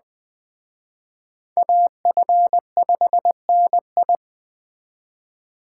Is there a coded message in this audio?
Yes: Morse code from 1.6 to 4.2 s, heard as a keyed tone, about 21 wpm, 700 Hz. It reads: AF5NI